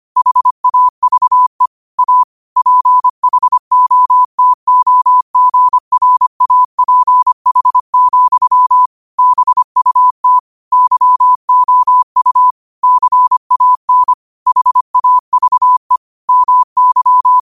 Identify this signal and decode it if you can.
Morse code, heard as a keyed tone, about 25 wpm, 1000 Hz. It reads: HAVE A PHOTOGRAPH, BUT YOU CAN HAVE MY